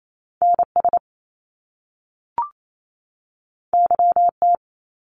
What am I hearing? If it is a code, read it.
Morse code, heard as a keyed tone, about 28 wpm, 700 Hz. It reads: NS YT
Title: Canadian Provinces and Territories 28wpm